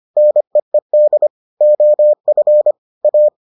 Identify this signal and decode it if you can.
Morse code, heard as a keyed tone, about 25 wpm, 600 Hz. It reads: NEED OF A